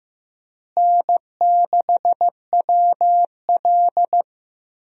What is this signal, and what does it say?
Morse code, heard as a keyed tone, about 15 wpm, 700 Hz. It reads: N6WL
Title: USA Call Signs - 4 Characters - Repeated 3x - 15wpm